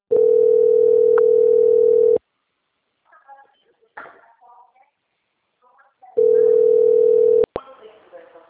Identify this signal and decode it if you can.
Ringback tone (North America), 440 + 480 Hz, with speech.
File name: out- - - - - .28434